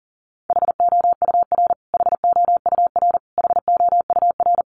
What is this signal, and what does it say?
Morse code, heard as a keyed tone, about 37 wpm, 700 Hz. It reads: HOUR HOUR HOUR
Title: Top 500 Words Repeated 40wpm